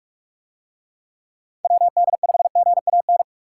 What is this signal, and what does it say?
Morse code, heard as a keyed tone, about 45 wpm, 700 Hz. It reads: WD5GAN